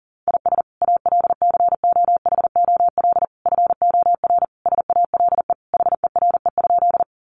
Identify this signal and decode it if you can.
Morse code, heard as a keyed tone, about 40 wpm, 700 Hz. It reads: IS ALCOHOL FOR SALE HERE?
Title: Sentences from Top 1400 Words 40wpm